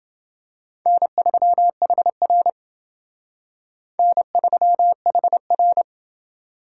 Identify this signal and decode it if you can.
Morse code, heard as a keyed tone, about 27 wpm, 700 Hz. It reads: N3HR N3HR